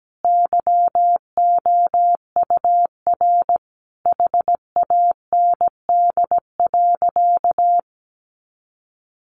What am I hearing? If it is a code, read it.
Morse code, heard as a keyed tone, about 17 wpm, 700 Hz. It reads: YOUR HAND.
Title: Sentences from Top 400 Words 17wpm